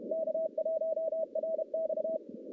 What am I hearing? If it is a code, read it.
Morse code, heard as a keyed tone, about 32 wpm, 640 Hz. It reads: K1RX
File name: contests / CW-SS-k5tr- -1200